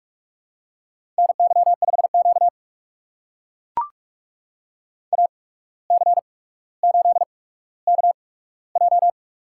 Morse code, heard as a keyed tone, about 44 wpm, spaced as for 18 wpm, 700 Hz. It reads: NY5Q AC8KJ
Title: USA Call Signs - 8x Character Spacing - 45wpm